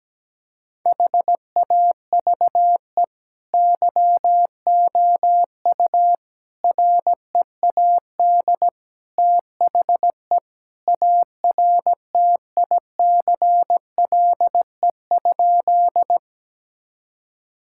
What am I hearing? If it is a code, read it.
Morse code, heard as a keyed tone, about 17 wpm, 700 Hz. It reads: HAVE YOU READ THE ARTICLE?
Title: Sentences from Top 1000 Words - Review Part 5 - 17wpm